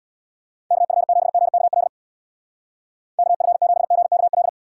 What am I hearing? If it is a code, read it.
Morse code, heard as a keyed tone, about 63 wpm, 700 Hz. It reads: KF7GKL KF7GKL